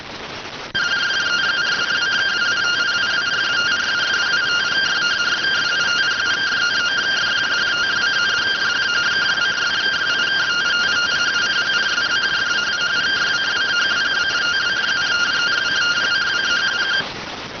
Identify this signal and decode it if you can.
Two-tone teleprinter signal, about 170 Hz shift.